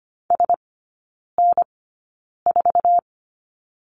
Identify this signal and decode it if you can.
Morse code, heard as a keyed tone, about 25 wpm, spaced as for 9 wpm, 690 Hz. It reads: SN4
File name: Lesson 14e - Number 2 - Three Characters 25wpm